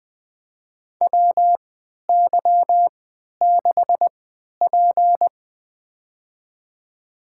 Morse code, heard as a keyed tone, about 20 wpm, spaced as for 11 wpm, 700 Hz. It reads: WY6P
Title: USA Call Signs - 4 Characters - 3x Character Spacing - 20wpm